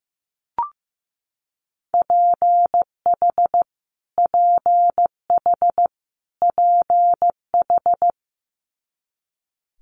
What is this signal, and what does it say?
Morse code, heard as a keyed tone, about 15 wpm, 700 Hz. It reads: PH PH PH
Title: Multi-letter Phonemes Repeated 3x 15wpm